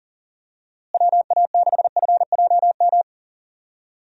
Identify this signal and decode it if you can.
Morse code, heard as a keyed tone, about 40 wpm, 700 Hz. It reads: WA6FJM
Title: USA Call Signs - 6 Characters - Repeated 2x - 40wpm